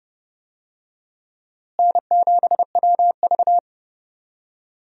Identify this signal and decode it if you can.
Morse code, heard as a keyed tone, about 32 wpm, 700 Hz. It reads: N7WV
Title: USA Call Signs - 4 Characters - Repeated 2x - 30wpm